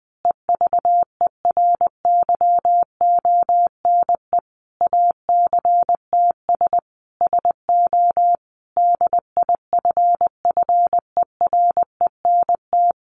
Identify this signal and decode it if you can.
Morse code, heard as a keyed tone, about 19 wpm, 700 Hz. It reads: EVERYONE ACTS SO DIFFERENT
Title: Easy Phrases 20wpm